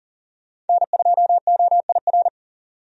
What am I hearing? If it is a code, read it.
Morse code, heard as a keyed tone, about 40 wpm, 700 Hz. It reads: N2OIR
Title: USA Call Signs - 5 Characters - 40wpm